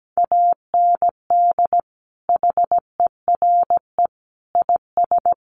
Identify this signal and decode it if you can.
Morse code, heard as a keyed tone, about 17 wpm, 700 Hz. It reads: AND HERE IS
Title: Sentences from Top 500 Words 17wpm